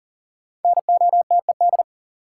Morse code, heard as a keyed tone, about 40 wpm, 700 Hz. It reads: NOTED
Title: Sets of 5 Words - New Word Drill - 40wpm